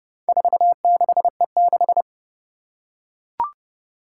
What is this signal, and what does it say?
Morse code, heard as a keyed tone, about 30 wpm, 700 Hz. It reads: 46E6
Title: Four Challenging Letter-Number Combinations 30wpm